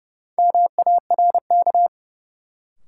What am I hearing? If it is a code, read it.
Morse code, heard as a keyed tone, about 32 wpm, 700 Hz. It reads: MARK
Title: Sentences from Top 300 Words - New Word Drill - 30wpm